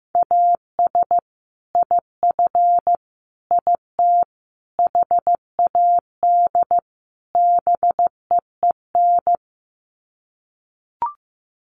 Morse code, heard as a keyed tone, about 15 wpm, 700 Hz. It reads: AS IF IT HAD BEEN
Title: Sets of 5 Words Encore2 15wpm